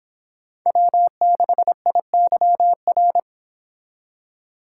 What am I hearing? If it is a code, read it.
Morse code, heard as a keyed tone, about 30 wpm, 700 Hz. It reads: W6IYR